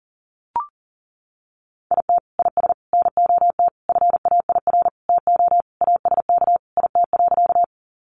Morse code, heard as a keyed tone, about 40 wpm, 700 Hz. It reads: IT IS NOT FAIR TO ASK IT.
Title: Sentences from Top 1300 Words 40wpm